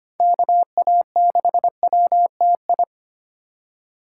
Morse code, heard as a keyed tone, about 25 wpm, 700 Hz. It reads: KA6WTI
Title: USA Call Signs - 6 Characters - 4x Character Spacing - 25wpm